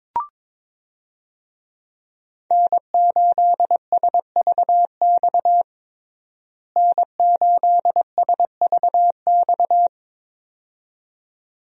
Morse code, heard as a keyed tone, about 22 wpm, 700 Hz. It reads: N8SVX N8SVX
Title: USA Call Signs - 5 Characters - Repeated 2x - 22wpm